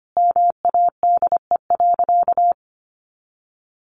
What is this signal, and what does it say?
Morse code, heard as a keyed tone, about 25 wpm, 700 Hz. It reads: MADE.
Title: Sentences from Top 1400 Words 25wpm